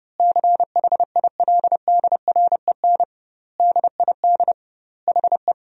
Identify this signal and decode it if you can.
Morse code, heard as a keyed tone, about 32 wpm, 700 Hz. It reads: CHILDREN DID HE
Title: 2x Word-Spacing - Sentences from Top 400 Words - 30wpm